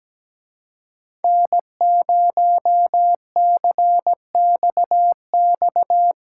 Morse code, heard as a keyed tone, about 18 wpm, 700 Hz. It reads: N0CXX
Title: Speed Racing - USA 1x3 Call Signs 17wpm